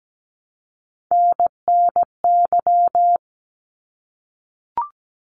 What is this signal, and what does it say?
Morse code, heard as a keyed tone, about 17 wpm, 700 Hz. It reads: NNY
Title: ARRL Sections 17wpm